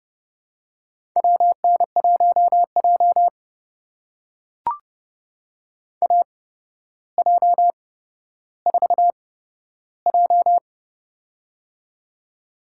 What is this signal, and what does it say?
Morse code, heard as a keyed tone, about 30 wpm, spaced as for 13 wpm, 700 Hz. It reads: WN1J AJ4J
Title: USA Call Signs - 4 Characters - 8x Character Spacing - 30wpm